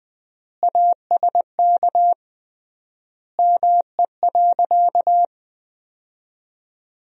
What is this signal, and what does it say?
Morse code, heard as a keyed tone, about 20 wpm, 700 Hz. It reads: ASK ME.
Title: 3x Word-Spacing - Sentences from Top 500 Words - 20wpm